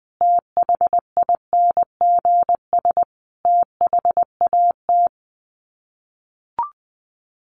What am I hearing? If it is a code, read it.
Morse code, heard as a keyed tone, about 20 wpm, 700 Hz. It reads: THINGS THAT
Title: Sets of 4 Words 20wpm